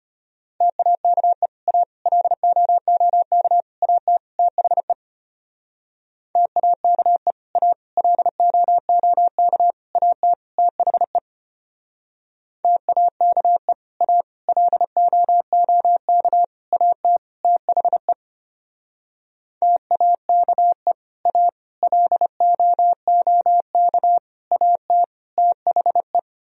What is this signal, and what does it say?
Morse code, heard as a keyed tone, about 31 wpm, 700 Hz. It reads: TAKE A LOOK AT THE TAKE A LOOK AT THE TAKE A LOOK AT THE TAKE A LOOK AT THE
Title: Speed-Racing - Sets of 5 Words 25wpm